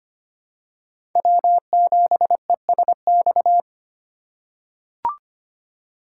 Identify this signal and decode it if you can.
Morse code, heard as a keyed tone, about 25 wpm, 700 Hz. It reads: W7ESX
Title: USA Call Signs - 5 Characters - 4x Character Spacing - 25wpm